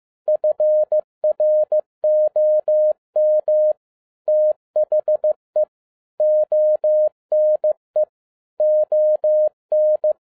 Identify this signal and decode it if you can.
Morse code, heard as a keyed tone, about 15 wpm, 610 Hz. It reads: FROM THE ONE ON
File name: 2011-10-mp3 / n02-FIELD DAY CONTEST QRV FROM SAKAE VILLAGE 15wpm